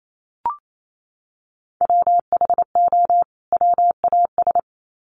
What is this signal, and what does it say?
Morse code, heard as a keyed tone, about 28 wpm, 700 Hz. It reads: WHO WAS
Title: Sets of 2 Words 28wpm